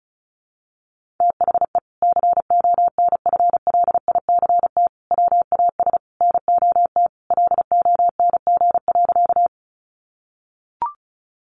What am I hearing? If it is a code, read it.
Morse code, heard as a keyed tone, about 34 wpm, 700 Hz. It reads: THE CONFLICT WAS NOT LONG.
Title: Sentences from Top 1300 Words 35wpm